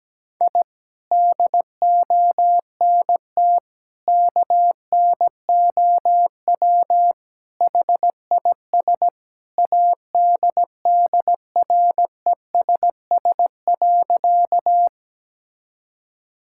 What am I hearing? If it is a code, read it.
Morse code, heard as a keyed tone, about 17 wpm, 700 Hz. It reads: I DONT KNOW HIS ADDRESS.
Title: Sentences from Top 1000 Words - Review Part 6 - 17wpm